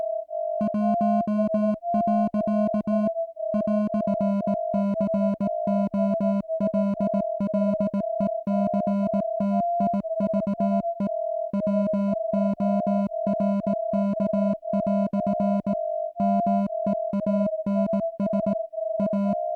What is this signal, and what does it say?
Morse code, heard as a keyed tone, about 18 wpm, 200 Hz. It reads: 1. "COLLECTIVE WORK" MEANS A